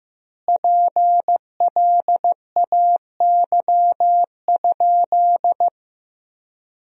Morse code, heard as a keyed tone, about 15 wpm, 700 Hz. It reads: PLAY?
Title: 4x Word-Spacing - Sentences from Top 200 Words - 15wpm